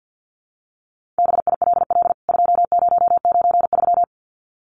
Morse code, heard as a keyed tone, about 50 wpm, 700 Hz. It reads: BILL 2093
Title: CWT Exchange 50wpm